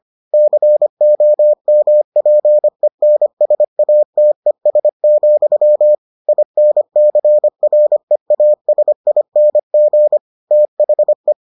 Morse code, heard as a keyed tone, about 25 wpm, 600 Hz. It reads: COMPENSATES, INCREASING THE